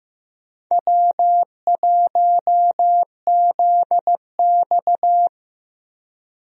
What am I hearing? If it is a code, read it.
Morse code, heard as a keyed tone, about 15 wpm, 700 Hz. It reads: W1ZX
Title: USA Call Signs - 4 Characters - Repeated 2x - 15wpm